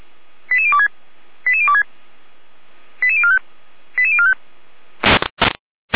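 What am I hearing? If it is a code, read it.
Five-tone selective call (ZVEI), heard as quick runs of five tones, tones of about 70 ms. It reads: 88026 88026 88045 88045
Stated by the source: ZVEI five-tone selective call, code 88105, received by SDR; fire brigade Feuerwehr Orlamünde.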